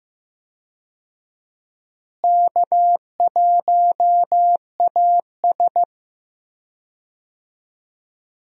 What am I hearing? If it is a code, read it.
Morse code, heard as a keyed tone, about 15 wpm, 700 Hz. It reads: K1AS
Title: USA Call Signs - 4 Characters - Repeated 3x - 15wpm